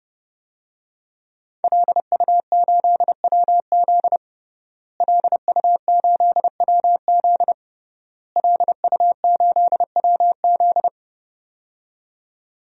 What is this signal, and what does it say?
Morse code, heard as a keyed tone, about 30 wpm, 700 Hz. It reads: LU8WZ LU8WZ LU8WZ
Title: DXCC-100 Call Signs - 5 Characters - Repeated 3x - 30wpm